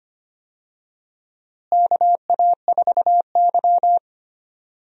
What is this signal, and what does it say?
Morse code, heard as a keyed tone, about 25 wpm, 700 Hz. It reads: KA4Y
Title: USA Call Signs - 4 Characters - Repeated 2x - 25wpm